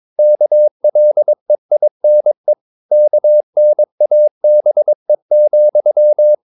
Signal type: Morse code